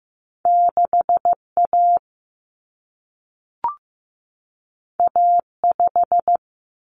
Morse code, heard as a keyed tone, about 15 wpm, 700 Hz. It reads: 6A A5
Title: Common Cut Numbers - 2-Digit 15wpm